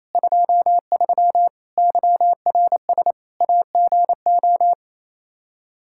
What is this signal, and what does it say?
Morse code, heard as a keyed tone, about 28 wpm, 700 Hz. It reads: 23 YRS AGO
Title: K1USN SST QSO Practice 28wpm